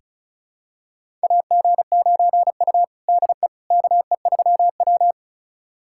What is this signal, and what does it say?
Morse code, heard as a keyed tone, about 36 wpm, 700 Hz. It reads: AG9U DE KE3W
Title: USA Callsign DE Callsign - 4 Characters - 35wpm